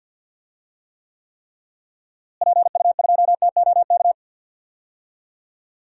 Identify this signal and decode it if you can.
Morse code, heard as a keyed tone, about 51 wpm, 700 Hz. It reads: PU2TOK
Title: DXCC-100 Call Signs - 6 Characters - Repeated 3x - 50wpm